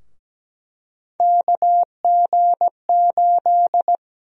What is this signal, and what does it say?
Morse code, heard as a keyed tone, about 17 wpm, 700 Hz. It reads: KG8
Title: USA Call Sign Prefixes - 17wpm